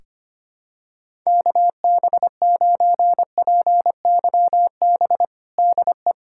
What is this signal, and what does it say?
Morse code, heard as a keyed tone, about 25 wpm, 700 Hz. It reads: KB9PYB DE
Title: USA Callsign DE Callsign 25wpm